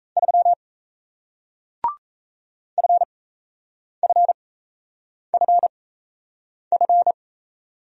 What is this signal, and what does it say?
Morse code, heard as a keyed tone, about 37 wpm, spaced as for 8 wpm, 700 Hz. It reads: 3 FFFF